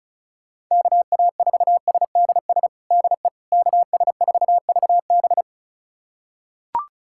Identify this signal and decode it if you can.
Morse code, heard as a keyed tone, about 36 wpm, 700 Hz. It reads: KA4SDS DE KS4VB
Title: USA Callsign DE Callsign 35wpm